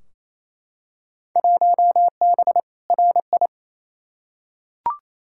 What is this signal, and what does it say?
Morse code, heard as a keyed tone, about 28 wpm, 700 Hz. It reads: 1B RI
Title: Field Day Exchanges Encore - 28wpm